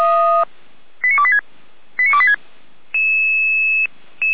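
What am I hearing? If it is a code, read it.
Five-tone selective call (ZVEI), heard as quick runs of five tones, tones of about 70 ms. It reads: 89286 89286
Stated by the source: ZVEI five-tone selective call 89119 received on SDR.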